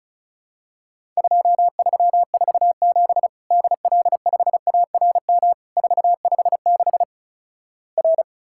Morse code, heard as a keyed tone, about 36 wpm, 700 Hz. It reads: 2347 DL5ARM 456 R
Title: Worked All Europe DX Contest - QTC Exchange - 35wpm